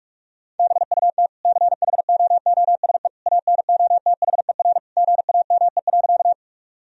Morse code, heard as a keyed tone, about 46 wpm, 700 Hz. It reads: BUT CHOOSE ANOTHER GAME.
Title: Sentences from Top 1000 Words - Review Part 4 - 45wpm